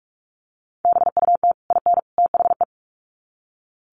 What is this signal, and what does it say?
Morse code, heard as a keyed tone, about 44 wpm, 700 Hz. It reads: BUT IN THE
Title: Sets of 3 Words 45wpm